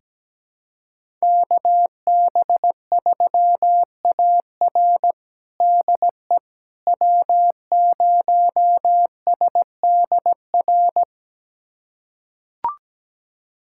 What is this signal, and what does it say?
Morse code, heard as a keyed tone, about 17 wpm, 700 Hz. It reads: KB3AR DE W0SDR
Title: USA Callsign DE Callsign - 5 Characters - 17wpm